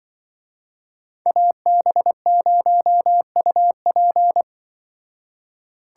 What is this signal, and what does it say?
Morse code, heard as a keyed tone, about 24 wpm, 700 Hz. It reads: AB0UP
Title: Speed Racing - USA Call Signs 20wpm